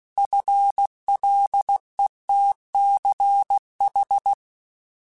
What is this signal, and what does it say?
Morse code, heard as a keyed tone, about 16 wpm, 800 Hz. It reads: FLETCH
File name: Alphabetical / Alphabetical mp3s / Fs @15